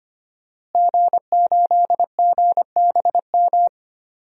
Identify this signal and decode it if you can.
Morse code, heard as a keyed tone, about 25 wpm, 700 Hz. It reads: G8GBM
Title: DXCC-100 Call Signs - 5 Characters - Repeated 2x - 25wpm